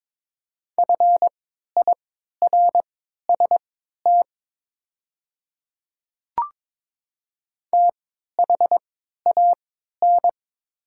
Morse code, heard as a keyed tone, about 21 wpm, spaced as for 12 wpm, 700 Hz. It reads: FIRST THAN
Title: 3x Character Spacing - Top 100 Words - 22wpm